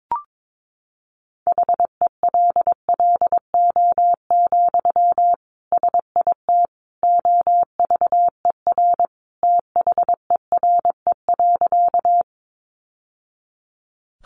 Morse code, heard as a keyed tone, about 22 wpm, 700 Hz. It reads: HELLO, SIT OVER THERE.
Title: Sentences from Top 1100 Words 22wpm